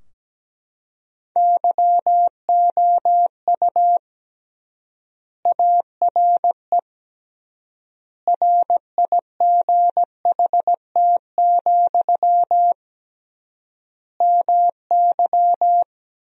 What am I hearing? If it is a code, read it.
Morse code, heard as a keyed tone, about 17 wpm, 700 Hz. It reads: YOU ARE RIGHT, MY
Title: 3x Word-Spacing - Sentences from Top 400 Words - 17wpm